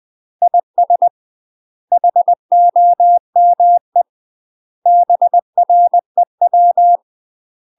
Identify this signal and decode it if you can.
Morse code, heard as a keyed tone, about 20 wpm, 700 Hz. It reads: IS HOME BREW